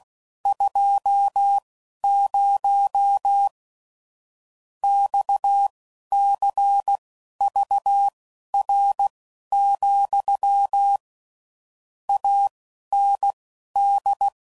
Morse code, heard as a keyed tone, about 16 wpm, spaced as for 11 wpm, 800 Hz. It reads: 20 XCVR, AND
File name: CW Code Groups @15WPM / QSOs / WD0FBT Spaced